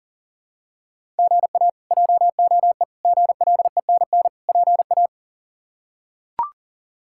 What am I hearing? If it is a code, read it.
Morse code, heard as a keyed tone, about 40 wpm, 700 Hz. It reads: GA JOE GLENN PA
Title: K1USN SST QSO Practice - 40wpm